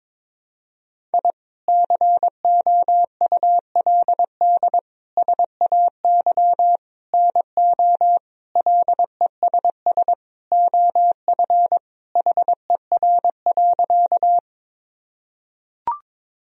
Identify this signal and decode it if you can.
Morse code, heard as a keyed tone, about 22 wpm, 700 Hz. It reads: I COULD SAY NO LESS OF HER.
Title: Sentences from Top 1000 Words - Review Part 2 - 22wpm